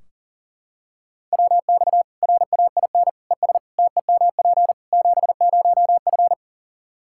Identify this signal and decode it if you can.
Morse code, heard as a keyed tone, about 37 wpm, 700 Hz. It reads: WX RAIN ES TEMP 70F
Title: Practice QSO - Exchange 2 - Info - 40wpm